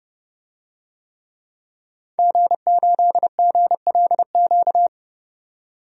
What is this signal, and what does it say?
Morse code, heard as a keyed tone, about 32 wpm, 700 Hz. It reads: G8GLQ
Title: DXCC-100 Call Signs - 5 Characters - 30wpm